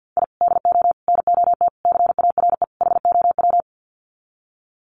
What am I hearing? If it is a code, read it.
Morse code, heard as a keyed tone, about 50 wpm, 700 Hz. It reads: I DO NOT CARE HOW
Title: Sets of 5 Words Encore2 50wpm